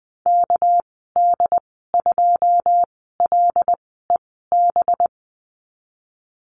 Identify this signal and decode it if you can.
Morse code, heard as a keyed tone, about 20 wpm, spaced as for 14 wpm, 700 Hz. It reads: KD2LEB
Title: 231 Lesson 37g - Letter x - Call Signs 20wpm